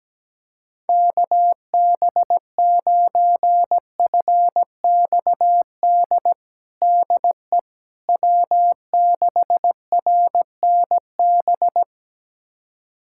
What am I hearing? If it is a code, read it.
Morse code, heard as a keyed tone, about 17 wpm, 700 Hz. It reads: KB9FXD DE W6RNB
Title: USA Callsign DE Callsign 17wpm